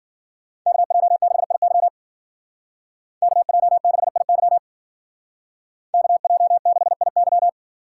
Morse code, heard as a keyed tone, about 52 wpm, 700 Hz. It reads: KJ6IY KJ6IY KJ6IY